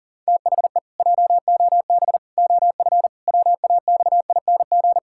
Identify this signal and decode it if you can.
Morse code, heard as a keyed tone, about 40 wpm, 700 Hz. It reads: THE JOB OF WAXING
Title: Pangram Challenge v2 - 40wpm